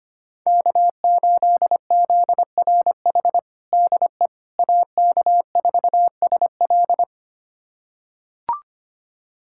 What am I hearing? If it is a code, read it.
Morse code, heard as a keyed tone, about 25 wpm, 700 Hz. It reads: K8ZRH DE AK4SL
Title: USA Callsign DE Callsign - 5 Characters - 25wpm